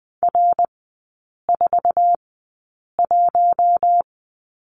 Morse code, heard as a keyed tone, about 20 wpm, spaced as for 8 wpm, 700 Hz. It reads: R41
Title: 067 Lesson 12e - Letter d - Three Characters 20wpm